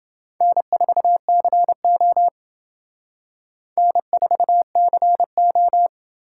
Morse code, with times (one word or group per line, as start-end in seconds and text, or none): 0.4-2.3 N4CO
3.8-5.9 N4CO